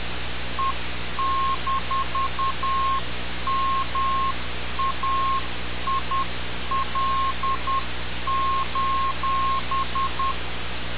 Morse code, heard as a keyed tone, about 10 wpm, 1070 Hz. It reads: E-MAIL:
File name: radiocap / REC z